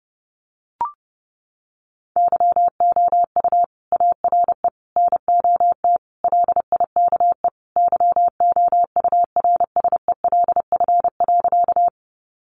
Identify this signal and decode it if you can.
Morse code, heard as a keyed tone, about 30 wpm, 700 Hz. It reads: YOU ARE NOT LIKE YOURSELF.
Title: Sentences from Top 600 Words 30wpm